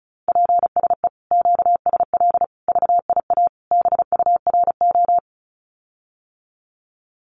Morse code, heard as a keyed tone, about 34 wpm, 700 Hz. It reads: PSE QSL VIA BURO
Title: Common QSO Phrases Encore 35wpm